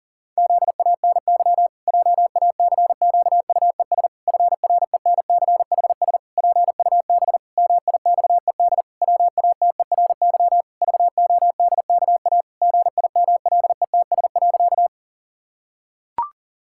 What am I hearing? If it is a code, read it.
Morse code, heard as a keyed tone, about 37 wpm, 700 Hz. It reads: ZANY JACQUES FRENCHS PUB MIXED WATERY VODKA GIMLETS.
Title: Pangram Challenge v2 - 40wpm